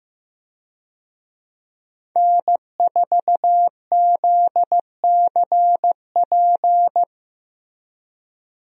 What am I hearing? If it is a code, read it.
Morse code, heard as a keyed tone, about 15 wpm, 700 Hz. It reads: N4ZCP
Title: USA Call Signs - 5 Characters - Repeated 2x - 15wpm